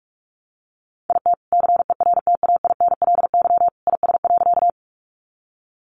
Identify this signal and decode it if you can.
Morse code, heard as a keyed tone, about 44 wpm, 700 Hz. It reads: IT CERTAINLY IS.
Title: Sentences from Top 900 Words 45wpm